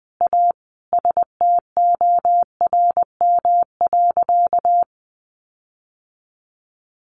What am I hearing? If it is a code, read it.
Morse code, heard as a keyed tone, about 20 wpm, 700 Hz. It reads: A STORM.
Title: Sentences from Top 1500 Words 20wpm